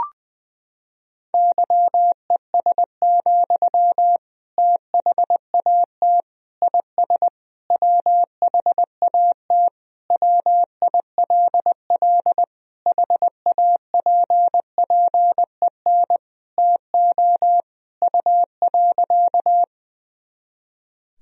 Morse code, heard as a keyed tone, about 21 wpm, 700 Hz. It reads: YES, THAT IS WHAT WILL HAPPEN TO U.
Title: Sentences from Top 500 Words and Abbreviations 20wpm